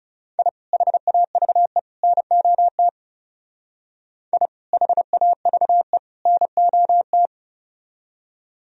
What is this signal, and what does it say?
Morse code, heard as a keyed tone, about 37 wpm, 700 Hz. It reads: I HAVE NOT I HAVE NOT
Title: Speed-Racing - Sets of 3 Words 30wpm